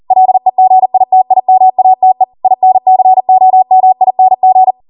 Morse code, heard as a keyed tone, about 40 wpm, 750 Hz. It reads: LEGITIMATE INCOMING
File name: Morse / 40WPM